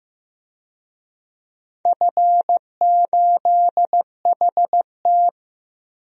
Morse code, heard as a keyed tone, about 15 wpm, 700 Hz. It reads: F8HT
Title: DXCC-100 Call Signs - 4 Characters - Repeated 3x - 15wpm